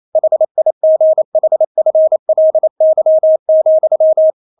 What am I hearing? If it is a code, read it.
Morse code, heard as a keyed tone, about 28 wpm, 630 Hz. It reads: HIGHFLY,
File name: Witz9 28wpm